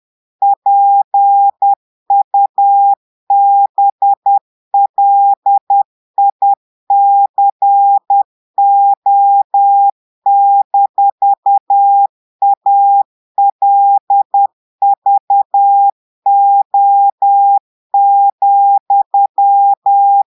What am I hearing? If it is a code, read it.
Morse code, heard as a keyed tone, about 10 wpm, 800 Hz. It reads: PUBLICO-ALVO,
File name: portugais texte en clair 2 vitesse 10 mots minutes